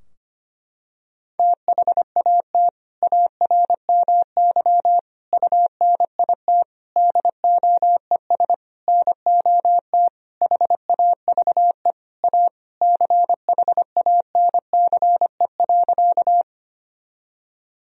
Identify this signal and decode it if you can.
Morse code, heard as a keyed tone, about 25 wpm, 700 Hz. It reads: THAT ARMY UNIT DOES NOT HAVE A CHANCE.
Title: Sentences from Top 1000 Words - Review Part 2 - 25wpm